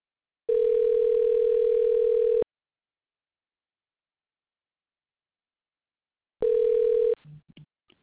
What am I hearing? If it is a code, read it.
Ringback tone (North America), 440 + 480 Hz.